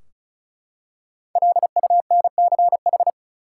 Morse code, heard as a keyed tone, about 36 wpm, 700 Hz. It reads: LUNCH
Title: Five Letter Words - 35wpm